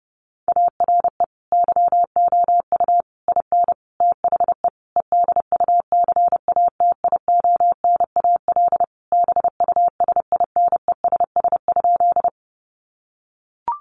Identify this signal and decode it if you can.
Morse code, heard as a keyed tone, about 30 wpm, 700 Hz. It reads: ARE YOU IN THE EDUCATIONAL BUSINESS?
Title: Sentences from Top 1200 Words 30wpm